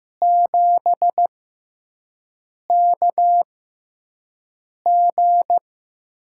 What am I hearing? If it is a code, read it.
Morse code, heard as a keyed tone, about 15 wpm, spaced as for 5 wpm, 700 Hz. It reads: 7KG
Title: USA Call Signs - 6x Character Spacing - 15wpm